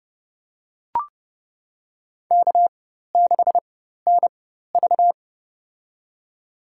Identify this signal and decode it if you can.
Morse code, heard as a keyed tone, about 30 wpm, spaced as for 14 wpm, 700 Hz. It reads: K6NV
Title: USA Call Signs - 4 Characters - 4x Character Spacing - 30wpm